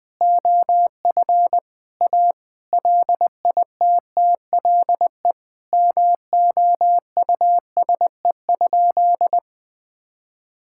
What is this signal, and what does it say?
Morse code, heard as a keyed tone, about 20 wpm, 700 Hz. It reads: OF A LITTLE MOUSE?
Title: Sentences from Top 1000 Words - Review Part 5 - 20wpm